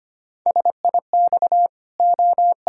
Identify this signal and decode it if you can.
Morse code, heard as a keyed tone, about 25 wpm, 700 Hz. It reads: SIX O
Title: Pangram Challenge v2 - 25wpm